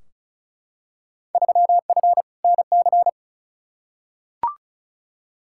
Morse code, heard as a keyed tone, about 36 wpm, 700 Hz. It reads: 3F NC
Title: Field Day Exchanges - 35wpm